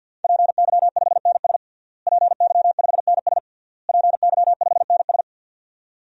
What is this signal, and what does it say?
Morse code, heard as a keyed tone, about 49 wpm, 700 Hz. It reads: PY5NS PY5NS PY5NS
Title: DXCC-100 Call Signs - 5 Characters - Repeated 3x - 50wpm